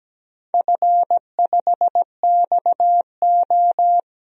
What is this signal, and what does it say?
Morse code, heard as a keyed tone, about 17 wpm, 700 Hz. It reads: F5XO
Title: DXCC-100 Call Signs - 4 Characters - Repeated 2x - 17wpm